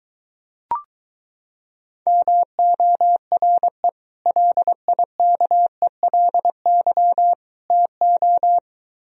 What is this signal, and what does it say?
Morse code, heard as a keyed tone, about 23 wpm, 700 Hz. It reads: MORE LIKELY TO